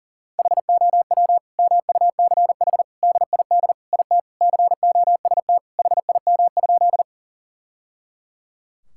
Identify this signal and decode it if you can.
Morse code, heard as a keyed tone, about 37 wpm, 690 Hz. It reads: HOW MUCH DID IT COST HIM?
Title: Sentences from Top 1000 Words - Review Part 5 - 40wpm